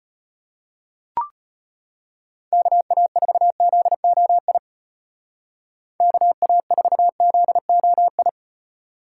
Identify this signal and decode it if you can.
Morse code, heard as a keyed tone, about 33 wpm, 700 Hz. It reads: KA4ZOI KA4ZOI